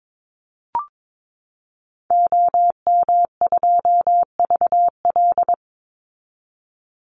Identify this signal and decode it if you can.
Morse code, heard as a keyed tone, about 22 wpm, 700 Hz. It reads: OM2VL
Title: Call Signs Repeated 3x - 22wpm